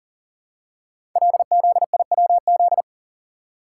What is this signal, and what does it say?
Morse code, heard as a keyed tone, about 39 wpm, 690 Hz. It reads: LZIWZ
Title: Code Groups - 5 Letters - 40wpm